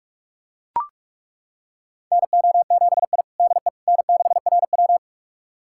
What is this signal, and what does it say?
Morse code, heard as a keyed tone, about 41 wpm, 690 Hz. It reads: NO7I DE N6RW
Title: USA Callsign DE Callsign - 4 Characters - 45wpm